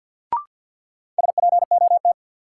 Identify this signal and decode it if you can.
Morse code, heard as a keyed tone, about 46 wpm, 700 Hz. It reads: SPOT
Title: Sentences from Top 1100 Words - New Word Drill Alt - 50wpm